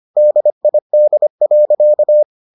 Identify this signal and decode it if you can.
Morse code, heard as a keyed tone, about 25 wpm, 600 Hz. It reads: DID.